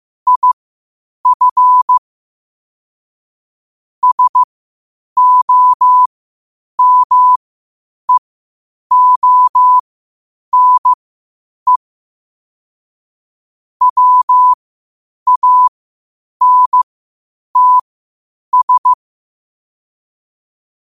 Morse code, heard as a keyed tone, about 15 wpm, spaced as for 8 wpm, 1000 Hz. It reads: IF SOMEONE WANTS
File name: qotdfiles / 8wpm